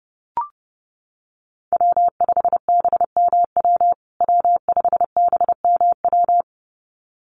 Morse code, heard as a keyed tone, about 30 wpm, 700 Hz. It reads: W5BMW W5BMW
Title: USA Call Signs Repeated 2x 30wpm